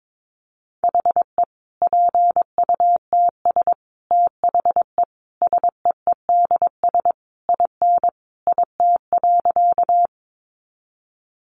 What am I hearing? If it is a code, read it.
Morse code, heard as a keyed tone, about 22 wpm, 700 Hz. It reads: HE PUTS THE SEEDS IN IT.
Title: Easy Phrases 22wpm